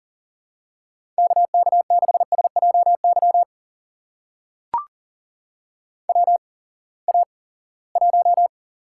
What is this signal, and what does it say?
Morse code, heard as a keyed tone, about 42 wpm, spaced as for 23 wpm, 700 Hz. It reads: KK6SJY WA1
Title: USA Call Signs - 6 Characters - 8x Character Spacing - 40wpm